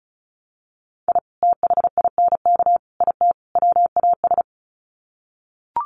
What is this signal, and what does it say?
Morse code, heard as a keyed tone, about 34 wpm, 700 Hz. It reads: I THINK IT WAS
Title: Sets of 4 Words 35wpm